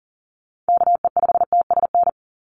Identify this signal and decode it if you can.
Morse code, heard as a keyed tone, about 40 wpm, 700 Hz. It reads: KE5TSN
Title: USA Call Signs 40wpm